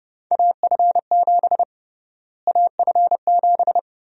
Morse code, heard as a keyed tone, about 30 wpm, 700 Hz. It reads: AF7 AF7
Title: USA Call Sign Prefixes - Repeated 2x - 30wpm